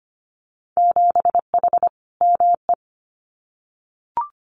Morse code, heard as a keyed tone, about 25 wpm, 700 Hz. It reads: 7H ME
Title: Winter Field Day Exchanges Encore 25wpm